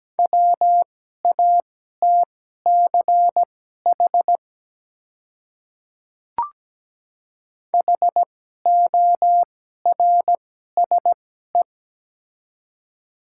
Morse code, heard as a keyed tone, about 17 wpm, spaced as for 12 wpm, 700 Hz. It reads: WATCH HORSE
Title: 2x Character Spacing - Top 300 Words - 17wpm